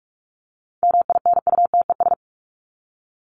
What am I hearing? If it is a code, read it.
Morse code, heard as a keyed tone, about 44 wpm, 700 Hz. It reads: MINUTES
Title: ES:S Suffix - Top 100 Words with Base Words 45wpm